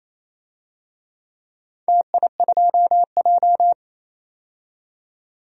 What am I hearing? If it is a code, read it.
Morse code, heard as a keyed tone, about 28 wpm, 700 Hz. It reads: TI2J
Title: DXCC-100 Call Signs - 4 Characters - 28wpm